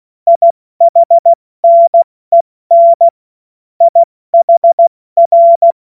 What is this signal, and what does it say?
Morse code, heard as a keyed tone, about 16 wpm, 680 Hz. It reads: IHNEN IHR
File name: Morsekurs / Witze / Witz 07. Dezember 2021, 70 F80